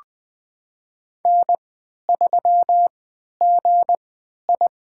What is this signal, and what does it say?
Morse code, heard as a keyed tone, about 20 wpm, spaced as for 11 wpm, 700 Hz. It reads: N3GI
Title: USA Call Signs - 3x Character Spacing - 20wpm